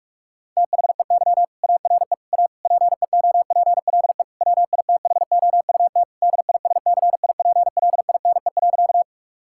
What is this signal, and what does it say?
Morse code, heard as a keyed tone, about 46 wpm, 700 Hz. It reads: THEY ARE A PEOPLE WITHOUT DISCIPLINE.
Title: Sentences from Top 1000 Words - Review Part 6 - 45wpm